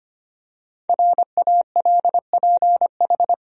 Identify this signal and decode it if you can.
Morse code, heard as a keyed tone, about 25 wpm, 700 Hz. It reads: RALPH
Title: USA Common Names - 25wpm